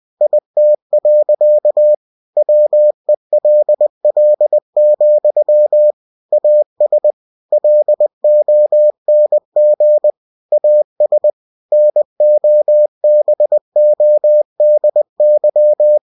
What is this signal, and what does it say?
Morse code, heard as a keyed tone, about 20 wpm, 600 Hz. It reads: IT. WELL, AS LONG AS NOBODY
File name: cw-books / winnie-the-pooh / winnie-20wpm-page-0088